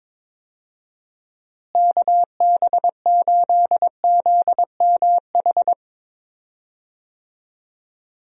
Morse code, heard as a keyed tone, about 22 wpm, 700 Hz. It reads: KB8ZMH
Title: USA Call Signs - 6 Characters - Repeated 3x - 22wpm